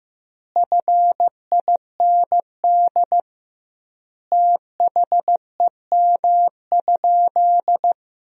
Morse code, heard as a keyed tone, about 16 wpm, 700 Hz. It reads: FIND THEM?
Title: 2x Word-Spacing - Sentences from Top 200 Words - 15wpm